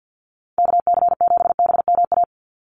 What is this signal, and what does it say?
Morse code, heard as a keyed tone, about 50 wpm, 700 Hz. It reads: KC7BMA
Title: USA Call Signs 50wpm